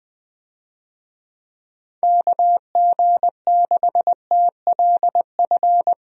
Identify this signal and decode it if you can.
Morse code, heard as a keyed tone, about 21 wpm, 700 Hz. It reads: KG6TLF
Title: USA Call Signs - 6 Characters - Repeated 3x - 20wpm